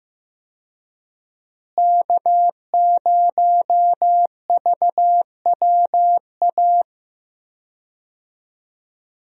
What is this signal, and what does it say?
Morse code, heard as a keyed tone, about 15 wpm, 700 Hz. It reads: K0VWA
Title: USA Call Signs - 5 Characters - Repeated 3x - 15wpm